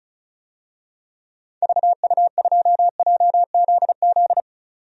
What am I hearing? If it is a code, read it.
Morse code, heard as a keyed tone, about 36 wpm, 700 Hz. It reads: VU2JZZ
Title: DXCC-100 Call Signs - 6 Characters - 35wpm